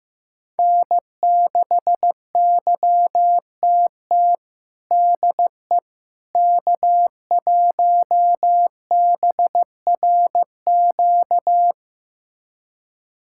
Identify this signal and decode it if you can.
Morse code, heard as a keyed tone, about 15 wpm, 700 Hz. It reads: N6YTT DE K1BRQ
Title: USA Callsign DE Callsign 15wpm